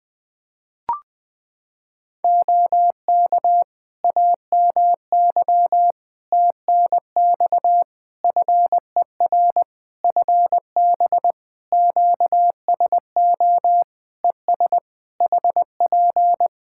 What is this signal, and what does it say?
Morse code, heard as a keyed tone, about 20 wpm, 700 Hz. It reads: OK AMY TNX FER FB QSO ES HP
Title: Practice QSO - Exchange 4 - Ending - 20wpm